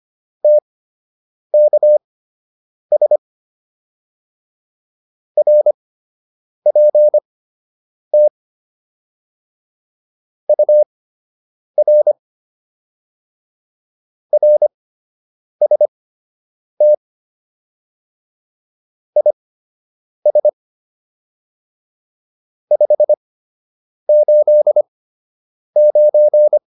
Morse code, heard as a keyed tone, about 25 wpm, spaced as for 8 wpm, 600 Hz. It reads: TKS RPT UR RST IS 589